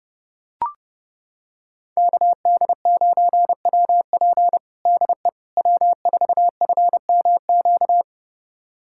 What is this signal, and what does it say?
Morse code, heard as a keyed tone, about 32 wpm, 700 Hz. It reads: KD9WP DE W4FMQ
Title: USA Callsign DE Callsign - 5 Characters - 30wpm